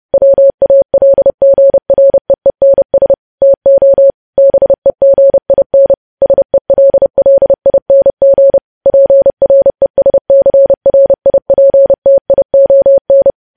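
Morse code, heard as a keyed tone, about 30 wpm, 550 Hz. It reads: WALGREENS TO BEGIN SELLING PRESCRIPTION
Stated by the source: Morse code news headlines sent at 30 words per minute.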